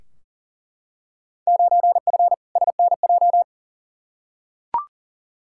Morse code, heard as a keyed tone, about 40 wpm, 700 Hz. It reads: 9F SNJ
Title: Field Day Exchanges Encore - 40wpm